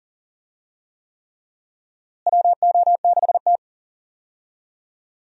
Morse code, heard as a keyed tone, about 39 wpm, 700 Hz. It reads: WO6T
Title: USA Call Signs - 4 Characters - Repeated 3x - 40wpm